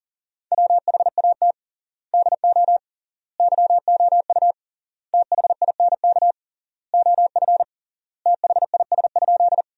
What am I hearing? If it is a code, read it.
Morse code, heard as a keyed tone, about 40 wpm, 700 Hz. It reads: WHAT DO YOU THINK OF THIS?
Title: 3x Word-Spacing - Sentences from Top 200 Words - 40wpm